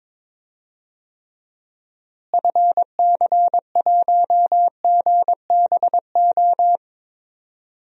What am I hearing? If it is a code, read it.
Morse code, heard as a keyed tone, about 22 wpm, 700 Hz. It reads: FC1GBO
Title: DXCC-100 Call Signs - 6 Characters - Repeated 2x - 22wpm